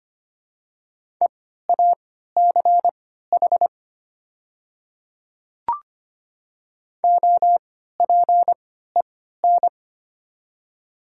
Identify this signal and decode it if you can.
Morse code, heard as a keyed tone, about 25 wpm, spaced as for 14 wpm, 700 Hz. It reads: EACH OPEN
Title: 3x Character Spacing - Top 300 Words - 25wpm